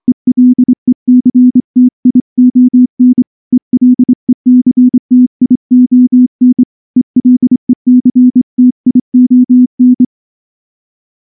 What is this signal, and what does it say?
Morse code, heard as a keyed tone, about 25 wpm, 250 Hz. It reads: ELECTION ELECTION ELECTION
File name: morse / morse-training-25WPM